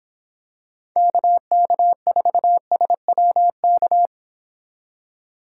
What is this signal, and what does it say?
Morse code, heard as a keyed tone, about 25 wpm, 700 Hz. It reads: KK4SWK